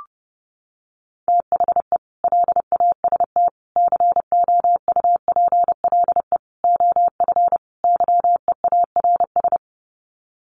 Morse code, heard as a keyed tone, about 30 wpm, 700 Hz. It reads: THE LAST COUPLE OF YEARS
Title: Sets of 5 Words 30wpm